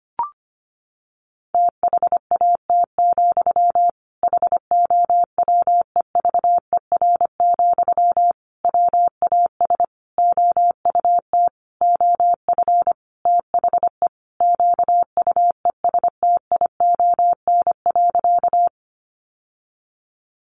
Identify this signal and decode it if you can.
Morse code, heard as a keyed tone, about 25 wpm, 700 Hz. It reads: THAT, HOWEVER, WAS OUT OF THE QUESTION.
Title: Sentences from Top 600 Words 25wpm